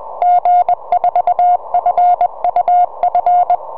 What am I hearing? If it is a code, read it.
Morse code, heard as a keyed tone, about 21 wpm, 720 Hz. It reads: G4FUF
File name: g4fuf 4m